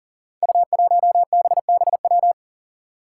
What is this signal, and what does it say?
Morse code, heard as a keyed tone, about 41 wpm, 700 Hz. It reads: U1BBW
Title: Five Challenging Letter-Number Combinations 40wpm